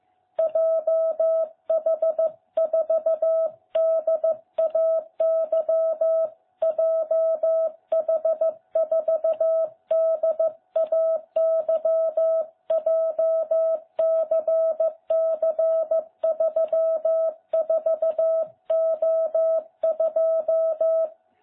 Morse code, heard as a keyed tone, about 15 wpm, 660 Hz. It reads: JH4DAY JH4DAY JCC34O2